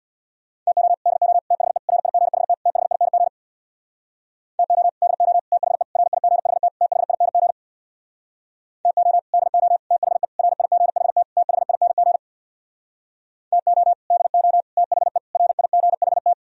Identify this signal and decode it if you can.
Morse code, heard as a keyed tone, about 64 wpm, 700 Hz. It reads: TO DO THE RIGHT THING TO DO THE RIGHT THING TO DO THE RIGHT THING TO DO THE RIGHT